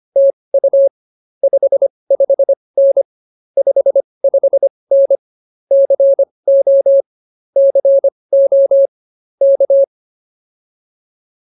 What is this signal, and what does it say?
Morse code, heard as a keyed tone, about 24 wpm, spaced as for 20 wpm, 550 Hz. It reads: TU 55N 55N CO CO K